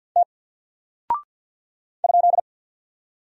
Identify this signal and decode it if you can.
Morse code, heard as a keyed tone, about 48 wpm, 700 Hz. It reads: T ?
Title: Single Character (Letter:Number:Punctuation:Prosign) Alt 50wpm